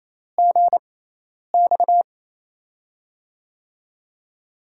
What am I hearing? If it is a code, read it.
Morse code, heard as a keyed tone, about 28 wpm, spaced as for 10 wpm, 700 Hz. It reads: GX
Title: USA Call Signs - 4 Characters - 6x Character Spacing - 28wpm